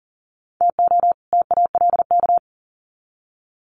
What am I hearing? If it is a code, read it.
Morse code, heard as a keyed tone, about 40 wpm, 700 Hz. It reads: TO TALK
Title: Sets of 2 Words 40wpm